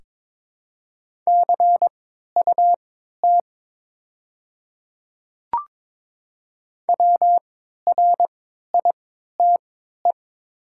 Morse code, heard as a keyed tone, about 22 wpm, spaced as for 13 wpm, 700 Hz. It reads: CUT WRITE
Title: 3x Character Spacing - Top 300 Words - 22wpm